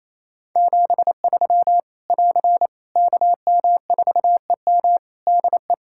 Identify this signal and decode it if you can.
Morse code, heard as a keyed tone, about 28 wpm, 700 Hz. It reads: 73 <AR> KM4EM DE
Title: Practice QSO - Exchange 4 - Ending - 28wpm